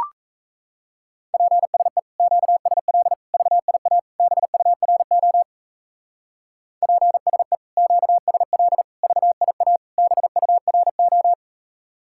Morse code, heard as a keyed tone, about 35 wpm, 700 Hz. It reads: PSE QSL VIA BURO PSE QSL VIA BURO
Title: Speed-Racing Common QSO Phrases Encore - 28wpm